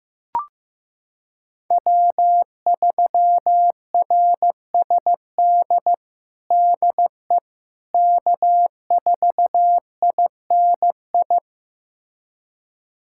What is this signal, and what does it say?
Morse code, heard as a keyed tone, about 15 wpm, 700 Hz. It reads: W3RSD DE K4INI
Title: USA Callsign DE Callsign - 5 Characters - 15wpm